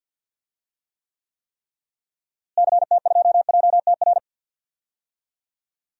Morse code, heard as a keyed tone, about 52 wpm, 700 Hz. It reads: CT2JTR
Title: DXCC-100 Call Signs - 6 Characters - Repeated 3x - 50wpm